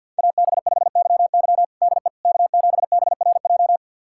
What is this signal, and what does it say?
Morse code, heard as a keyed tone, about 51 wpm, 700 Hz. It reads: AB5YY DE K7BRJ
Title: USA Callsign DE Callsign - 5 Characters - 50wpm